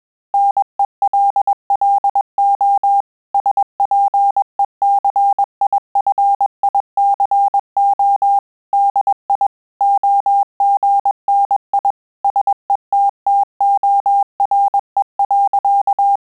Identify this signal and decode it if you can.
Morse code, heard as a keyed tone, about 21 wpm, 800 Hz. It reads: NELLO SPECIFICO DI OGNI SETTORE.
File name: Telegraphy-group / breve testo midi